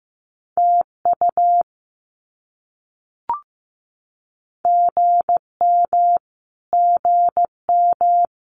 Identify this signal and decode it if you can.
Morse code, heard as a keyed tone, about 15 wpm, 700 Hz. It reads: TU GM GM
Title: Abbreviations Repeated 3x - 15wpm